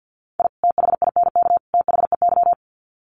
Morse code, heard as a keyed tone, about 50 wpm, 700 Hz. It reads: I THINK THEY
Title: Sets of 3 Words 50wpm